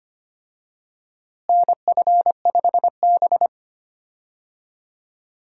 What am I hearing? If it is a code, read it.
Morse code, heard as a keyed tone, about 25 wpm, 700 Hz. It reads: NF5B
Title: USA Call Signs - 4 Characters - Repeated 2x - 25wpm